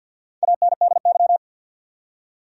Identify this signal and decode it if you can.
Morse code, heard as a keyed tone, about 53 wpm, 700 Hz. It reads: ANDY
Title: USA Common Names - Max 4 Letters - 50wpm